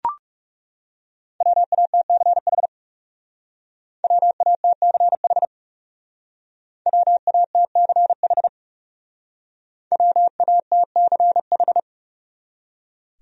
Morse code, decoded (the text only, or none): WATCH WATCH WATCH WATCH